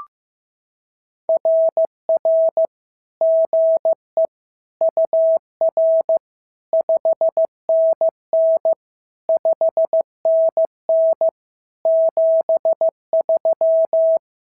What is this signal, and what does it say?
Morse code, heard as a keyed tone, about 15 wpm, 650 Hz. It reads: RR GE UR 5NN 5NN 73
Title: SOTA QSO Practice 15wpm